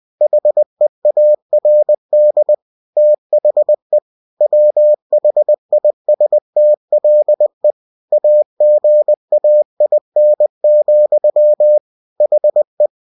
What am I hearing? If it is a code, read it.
Morse code, heard as a keyed tone, about 20 wpm, 600 Hz. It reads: HEARD THE WHISTLE AGAIN, HE
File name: cw-books / winnie-the-pooh / winnie-20wpm-page-0024